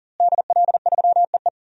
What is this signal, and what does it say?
Morse code, heard as a keyed tone, about 40 wpm, 700 Hz. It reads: DL3EE
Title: DXCC-100 Call Signs - 5 Characters - Repeated 3x - 40wpm